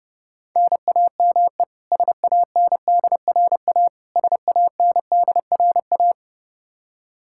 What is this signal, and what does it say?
Morse code, heard as a keyed tone, about 32 wpm, 700 Hz. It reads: NAME SANDRA SANDRA
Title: Practice QSO - Exchange 1 - Beginning - 30wpm